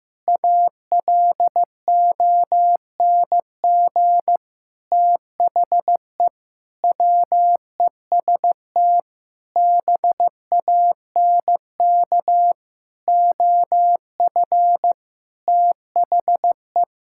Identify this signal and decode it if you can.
Morse code, heard as a keyed tone, about 15 wpm, 700 Hz. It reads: ALONG THE WEST BANK OF THE
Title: Sentences from Top 1000 Words - Review Part 6 - 15wpm